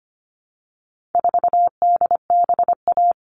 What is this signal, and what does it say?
Morse code, heard as a keyed tone, about 25 wpm, 700 Hz. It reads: 4DBA
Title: All Cut Numbers - 4-Digit 25wpm